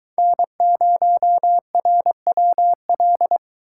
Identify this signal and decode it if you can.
Morse code, heard as a keyed tone, about 22 wpm, 700 Hz. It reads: N0RWL